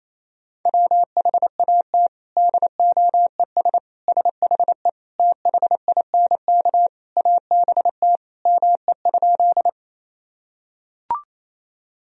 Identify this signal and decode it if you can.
Morse code, heard as a keyed tone, about 28 wpm, 700 Hz. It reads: WHAT DOES SHE THINK ABT ME?
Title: Sentences from Top 200 Words and Abbreviations 28wpm